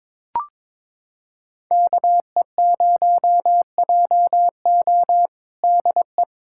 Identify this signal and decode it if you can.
Morse code, heard as a keyed tone, about 21 wpm, 700 Hz. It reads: KE0JO DE
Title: USA Callsign DE Callsign - 5 Characters - 22wpm